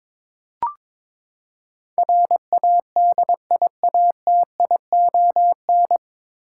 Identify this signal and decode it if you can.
Morse code, heard as a keyed tone, about 21 wpm, 700 Hz. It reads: RADIATION
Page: Morse Code Ninja practice 22wpm